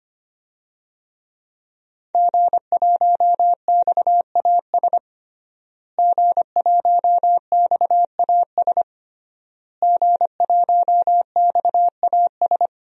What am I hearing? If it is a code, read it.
Morse code, heard as a keyed tone, about 25 wpm, 700 Hz. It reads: G1XAS G1XAS G1XAS
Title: DXCC-100 Call Signs - 5 Characters - Repeated 3x - 25wpm